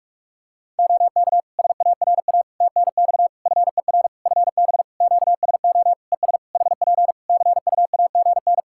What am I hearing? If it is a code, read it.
Morse code, heard as a keyed tone, about 46 wpm, 700 Hz. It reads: OK SARA TNX FER FB QSO ES HP CUAGN
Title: Practice QSO - Exchange 4 - Ending - 45wpm